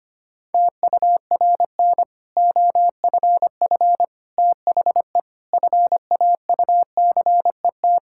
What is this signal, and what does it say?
Morse code, heard as a keyed tone, about 25 wpm, 700 Hz. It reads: TURN OFF THE FAUCET
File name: practice / 25wpm / Difficult Sentences with Hints - 25wpm